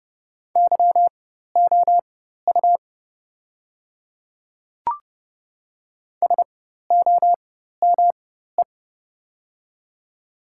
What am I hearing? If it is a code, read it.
Morse code, heard as a keyed tone, about 28 wpm, spaced as for 14 wpm, 700 Hz. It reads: YOU SOME
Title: 4x Character Spacing - Top 100 Words - 30wpm